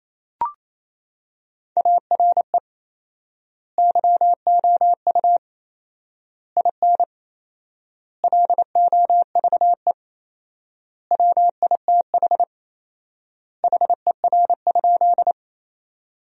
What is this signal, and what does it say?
Morse code, heard as a keyed tone, about 28 wpm, 700 Hz. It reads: ARE YOU IN LOVE WITH HER?
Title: 4x Word-Spacing - Sentences from Top 400 Words - 28wpm